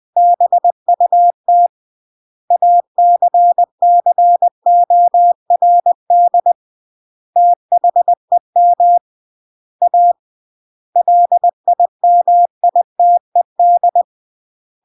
Morse code, heard as a keyed tone, about 20 wpm, 700 Hz. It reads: BUT ACCORD THEM A LIMITED
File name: CW / jodorowsky / 82 commandments 20-20